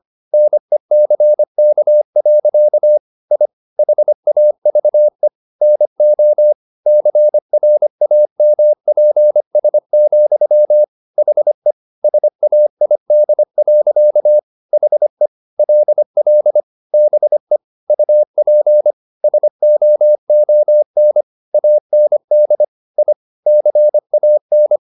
Morse code, heard as a keyed tone, about 25 wpm, 600 Hz. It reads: NECK. I HAVE NO CRAMPS, HE SAID. HE LL BE UP SOON AND I CAN